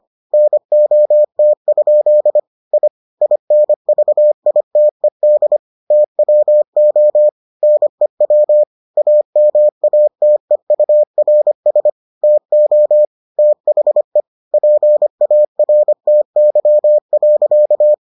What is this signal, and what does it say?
Morse code, heard as a keyed tone, about 25 wpm, 600 Hz. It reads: NOT? I INVITED TWO NEW AMATEURS TO THE PARTY.